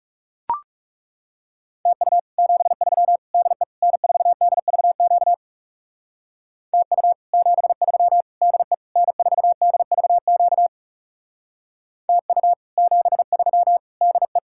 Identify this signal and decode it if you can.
Morse code, heard as a keyed tone, about 37 wpm, 700 Hz. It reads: TU 73 DE N4DVQ TU 73 DE N4DVQ TU 73 DE
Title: Speed-Racing Common QSO Phrases - 30wpm